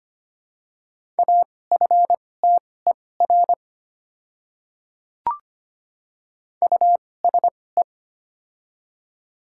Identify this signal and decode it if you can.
Morse code, heard as a keyed tone, about 25 wpm, spaced as for 18 wpm, 700 Hz. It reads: AFTER USE
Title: 2x Character Spacing - Top 500 Words - 25wpm